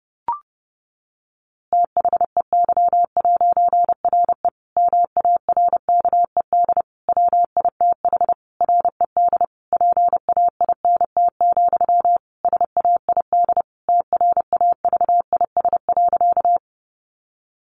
Morse code, heard as a keyed tone, about 30 wpm, 700 Hz. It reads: THEY'RE MARKED WITH RED PAINT, SAID TRAVIS.
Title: A Sound Of Thunder 30wpm